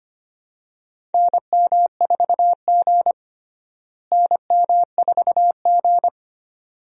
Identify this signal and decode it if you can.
Morse code, heard as a keyed tone, about 25 wpm, 700 Hz. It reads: NM4G NM4G
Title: USA Call Signs - 4 Characters - Repeated 2x - 25wpm